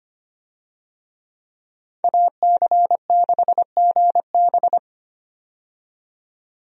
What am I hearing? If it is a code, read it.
Morse code, heard as a keyed tone, about 25 wpm, 690 Hz. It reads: AC6GB
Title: USA Call Signs - 5 Characters - 25wpm